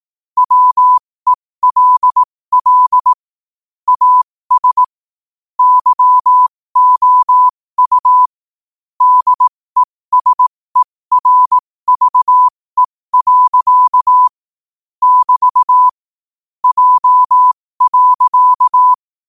Morse code, heard as a keyed tone, about 18 wpm, spaced as for 15 wpm, 1000 Hz. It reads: WELL AS YOU DESERVE. <BT> J.
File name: qotdfiles / 15wpm